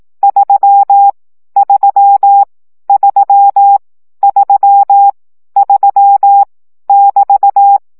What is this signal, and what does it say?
Morse code, heard as a keyed tone, about 18 wpm, 800 Hz. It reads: 3 3 3 3 3 <BT>
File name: CW gyakorlas 3 15 baud